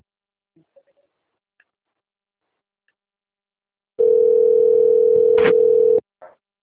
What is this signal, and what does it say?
Ringback tone (North America), 440 + 480 Hz.